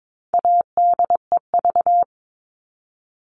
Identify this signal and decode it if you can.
Morse code, heard as a keyed tone, about 22 wpm, 700 Hz. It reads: ADEV
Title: All Cut Numbers - 4-Digit 22wpm